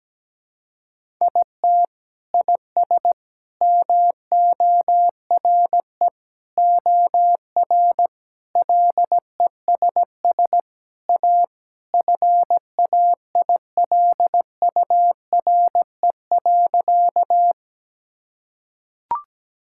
Morse code, heard as a keyed tone, about 17 wpm, 700 Hz. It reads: IT IS MORE OR LESS A FAILURE.
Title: Sentences from Top 1000 Words - Review Part 6 - 17wpm